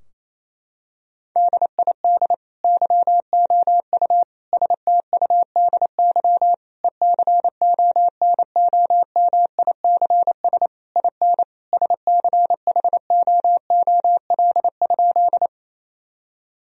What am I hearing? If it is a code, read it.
Morse code, heard as a keyed tone, about 27 wpm, 700 Hz. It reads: DID YOU STUDY ECONOMICS IN SCHOOL?
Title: Sentences from Top 1000 Words - Review Part 3 - 28wpm